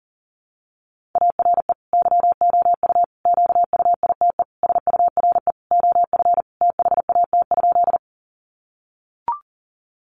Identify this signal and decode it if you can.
Morse code, heard as a keyed tone, about 37 wpm, 700 Hz. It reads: ARE YOU QUITE SURE OF THAT?
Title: Sentences from Top 700 Words 40wpm